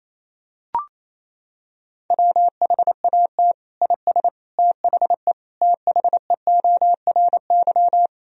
Morse code, heard as a keyed tone, about 28 wpm, 700 Hz. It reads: WHAT IS THE THEORY
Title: Sentences from Top 1000 Words - Review Part 3 - 28wpm